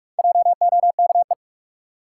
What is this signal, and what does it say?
Morse code, heard as a keyed tone, about 46 wpm, 700 Hz. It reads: JOKE
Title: Four Letter Words - 45wpm